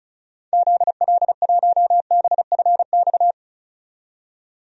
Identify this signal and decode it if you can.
Morse code, heard as a keyed tone, about 34 wpm, 700 Hz. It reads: ZL1BFX
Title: DXCC-100 Call Signs - 6 Characters - 35wpm